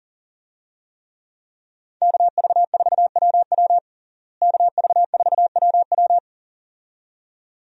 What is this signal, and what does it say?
Morse code, heard as a keyed tone, about 40 wpm, 700 Hz. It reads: KV4WW KV4WW
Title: USA Call Signs - 5 Characters - Repeated 2x - 40wpm